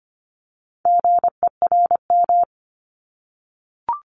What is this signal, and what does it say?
Morse code, heard as a keyed tone, about 25 wpm, 700 Hz. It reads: GERM
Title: Multi-letter Phonemes 25wpm